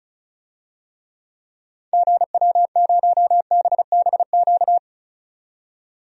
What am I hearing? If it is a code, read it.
Morse code, heard as a keyed tone, about 36 wpm, 700 Hz. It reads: GW0BBQ
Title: DXCC-100 Call Signs - 6 Characters - 35wpm